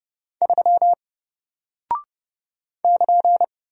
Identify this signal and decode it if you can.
Morse code, heard as a keyed tone, about 30 wpm, 700 Hz. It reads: 3 <KN>
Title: Single Character (Letter:Number:Punctuation:Prosign) Alt 30wpm